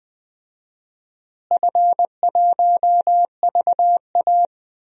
Morse code, heard as a keyed tone, about 20 wpm, 700 Hz. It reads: F1VA
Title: DXCC-100 Call Signs - 4 Characters - Repeated 3x - 20wpm